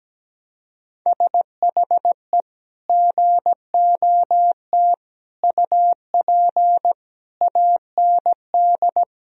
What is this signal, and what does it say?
Morse code, heard as a keyed tone, about 17 wpm, 700 Hz. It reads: SHE GOT UP AND
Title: Sentences from Top 1000 Words - Review Part 3 - 17wpm